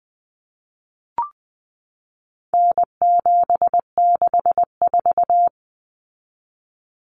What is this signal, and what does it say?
Morse code, heard as a keyed tone, about 20 wpm, 700 Hz. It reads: N764
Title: Common Cut Numbers - 4-Digit 20wpm